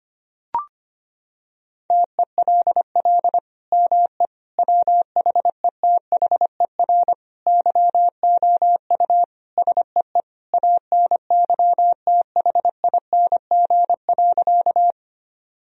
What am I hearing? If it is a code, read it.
Morse code, heard as a keyed tone, about 25 wpm, 700 Hz. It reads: TELL ME WHETHER YOU SEE ANYTHING.
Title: Sentences from Top 1000 Words - Review Part 3 - 25wpm